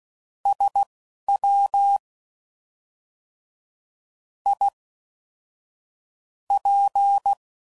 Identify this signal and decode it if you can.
Morse code, heard as a keyed tone, about 16 wpm, spaced as for 5 wpm, 800 Hz. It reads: SWIP